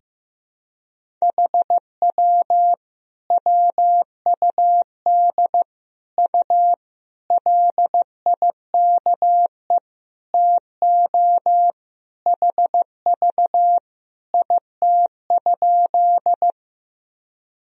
Morse code, heard as a keyed tone, about 15 wpm, 700 Hz. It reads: HW WUD U LIKE TO HV IT?
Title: Sentences from Top 100 Words and Abbreviations 15wpm